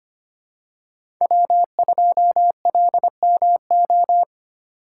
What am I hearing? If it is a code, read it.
Morse code, heard as a keyed tone, about 25 wpm, 700 Hz. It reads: W2LMO
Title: USA Call Signs - 5 Characters - 2x Character Spacing - 25wpm